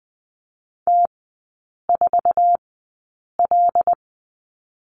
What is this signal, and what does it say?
Morse code, heard as a keyed tone, about 20 wpm, spaced as for 8 wpm, 700 Hz. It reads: T4L
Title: 133 Lesson 22e - Question Mark - Three Characters 20wpm